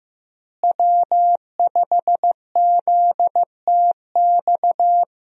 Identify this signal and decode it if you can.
Morse code, heard as a keyed tone, about 15 wpm, 700 Hz. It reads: W5ZTX
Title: USA Call Signs - 5 Characters - Repeated 2x - 15wpm